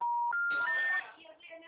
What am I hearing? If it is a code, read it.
Special information tone, with speech.
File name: out- -2016- - - .57065